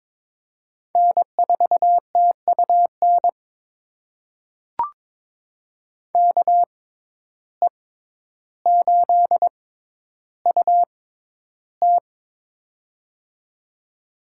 Morse code, heard as a keyed tone, about 22 wpm, spaced as for 11 wpm, 700 Hz. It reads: N4TUN KE8UT
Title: USA Call Signs - 5 Characters - 6x Character Spacing - 22wpm